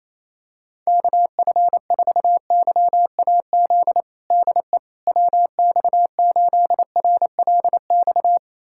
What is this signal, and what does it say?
Morse code, heard as a keyed tone, about 28 wpm, 700 Hz. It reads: KF4YAZ DE WX8RLX
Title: USA Callsign DE Callsign 28wpm